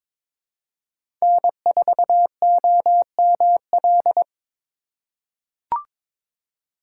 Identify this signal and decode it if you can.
Morse code, heard as a keyed tone, about 22 wpm, 700 Hz. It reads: N4OML
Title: USA Call Signs - 5 Characters - 4x Character Spacing - 22wpm